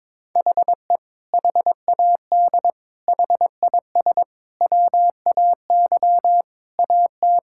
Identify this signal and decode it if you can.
Morse code, heard as a keyed tone, about 22 wpm, 700 Hz. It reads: HE HAD HIS WAY AT
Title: Sentences from Top 1600 Words 22wpm